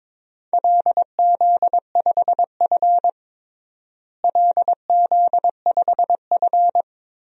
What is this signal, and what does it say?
Morse code, heard as a keyed tone, about 22 wpm, 700 Hz. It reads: LZ5F LZ5F
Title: DXCC-100 Call Signs - 4 Characters - Repeated 3x - 22wpm